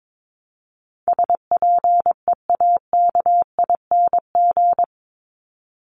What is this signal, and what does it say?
Morse code, heard as a keyed tone, about 22 wpm, 700 Hz. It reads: SPEAKING
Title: ING Suffix - Top 100 Words with Base Words 22wpm